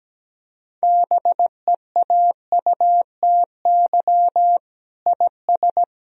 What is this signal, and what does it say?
Morse code, heard as a keyed tone, about 17 wpm, 700 Hz. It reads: BEAUTY IS
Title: Sentences from Top 1000 Words - Review Part 4 - 17wpm